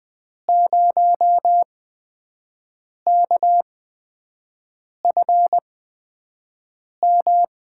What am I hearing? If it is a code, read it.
Morse code, heard as a keyed tone, about 20 wpm, spaced as for 5 wpm, 700 Hz. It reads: 0KFM
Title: USA Call Signs - 6 Characters - 8x Character Spacing - 20wpm